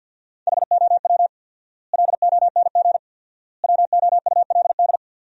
Morse code, heard as a keyed tone, about 53 wpm, 700 Hz. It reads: HOW LONG WOULD
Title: 4x Word-Spacing - Sentences from Top 500 Words - 50wpm